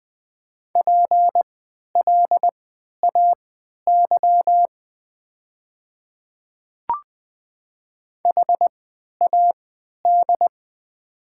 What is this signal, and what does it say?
Morse code, heard as a keyed tone, about 21 wpm, spaced as for 11 wpm, 700 Hz. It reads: PLAY HAD
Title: 3x Character Spacing - Top 200 Words - 20wpm